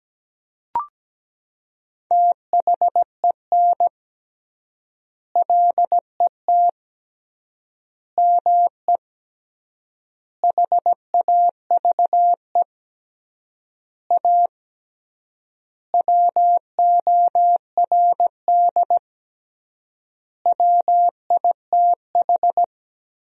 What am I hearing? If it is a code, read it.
Morse code, heard as a keyed tone, about 17 wpm, 700 Hz. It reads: THEN LET ME HAVE A WORD WITH
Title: 3x Word-Spacing - Sentences from Top 300 Words - 17wpm